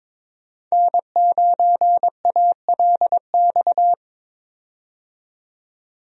Morse code, heard as a keyed tone, about 22 wpm, 700 Hz. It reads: N9ALX
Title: USA Call Signs - 5 Characters - 22wpm